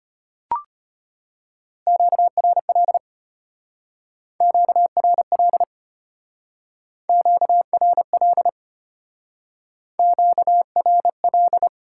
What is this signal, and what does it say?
Morse code, heard as a keyed tone, about 33 wpm, 690 Hz. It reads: QRL QRL QRL QRL